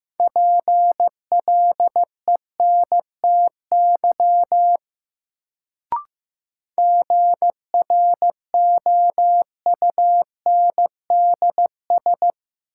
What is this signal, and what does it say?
Morse code, heard as a keyed tone, about 15 wpm, 700 Hz. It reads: PLENTY GROUNDS
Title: Sentences from Top 1300 Words - New Word Drill Alt - 15wpm